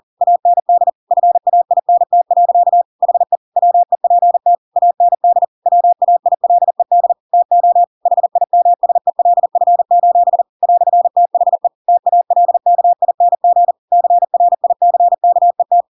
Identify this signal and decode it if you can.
Morse code, heard as a keyed tone, about 40 wpm, 700 Hz. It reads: AND FAINT. HE WEPT AND WAILED TO HIMSELF: "THE TALKING CRICKET
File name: Pinocchio / Pinocchio ch5 40 40 ik1hge